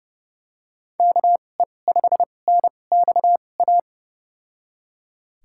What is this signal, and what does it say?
Morse code, heard as a keyed tone, about 30 wpm, spaced as for 22 wpm, 700 Hz. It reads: KE5NXA
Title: USA Call Signs - 2x Character Spacing - 30wpm